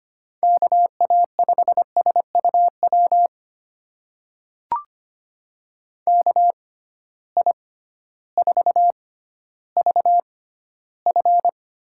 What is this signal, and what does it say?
Morse code, heard as a keyed tone, about 25 wpm, spaced as for 14 wpm, 700 Hz. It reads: KA5SUW KI4VF
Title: USA Call Signs - 6 Characters - 6x Character Spacing - 25wpm